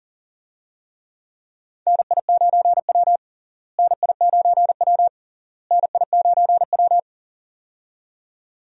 Morse code, heard as a keyed tone, about 40 wpm, 700 Hz. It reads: NI9W NI9W NI9W
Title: USA Call Signs - 4 Characters - Repeated 3x - 40wpm